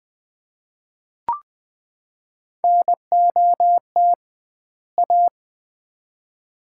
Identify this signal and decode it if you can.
Morse code, heard as a keyed tone, about 20 wpm, 700 Hz. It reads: NOT A
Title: 2x Word-Spacing - Sets of 2 Words - 20wpm